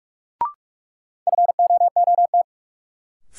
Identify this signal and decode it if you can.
Morse code, heard as a keyed tone, about 46 wpm, 700 Hz. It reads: FOOT
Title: Sentences from Top 500 Words - New Word Drill - 45wpm